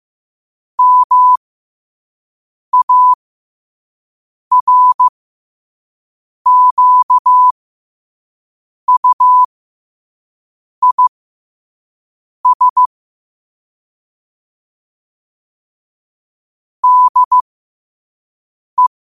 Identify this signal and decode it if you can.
Morse code, heard as a keyed tone, about 15 wpm, spaced as for 5 wpm, 1000 Hz. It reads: MARQUIS DE
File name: qotdfiles / 5wpm